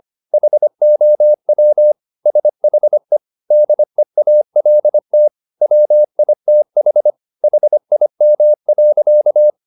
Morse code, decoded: HOW SHE DEALT WITH HIM.

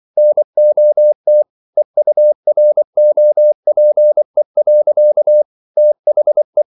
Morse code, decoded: NOT EUROPE. THE

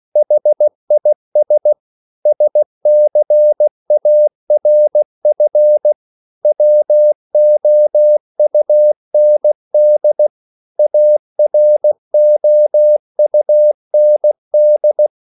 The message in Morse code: HIS SCARF WOUND AROUND

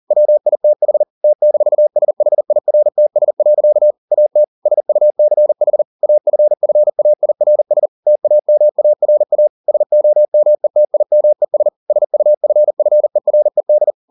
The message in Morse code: WITH T-SHIRTS. AT SUCH AFFAIRS TAMARA SOMETIMES SUFFERED

